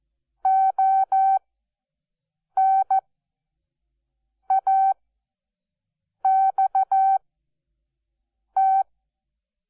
Morse code, heard as a keyed tone, about 14 wpm, spaced as for 5 wpm, 770 Hz. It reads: ONAXT
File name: Onlinelehrgang / morsen / DJ4UF Morsen12